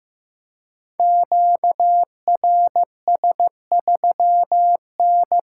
Morse code, heard as a keyed tone, about 15 wpm, 700 Hz. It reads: QRS3N
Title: Code Groups - 5 Letter:Numbers - 15wpm